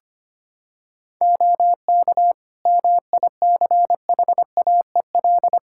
Morse code, heard as a keyed tone, about 25 wpm, 700 Hz. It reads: OK MICHAEL
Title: Practice QSO - Exchange 4 - Ending - 25wpm